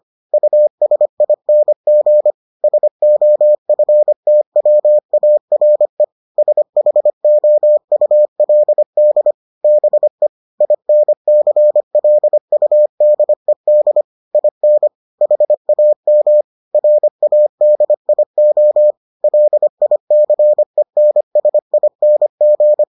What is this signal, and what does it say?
Morse code, heard as a keyed tone, about 25 wpm, 600 Hz. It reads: USING SOFTWARE SHOULD BE INCLUDED IN HAM RADIO LICENSING